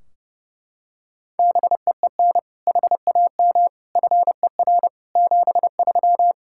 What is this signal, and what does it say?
Morse code, heard as a keyed tone, about 32 wpm, 700 Hz. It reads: BEEN HAM FER 73
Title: Practice QSO - Exchange 3 - More Info - 30wpm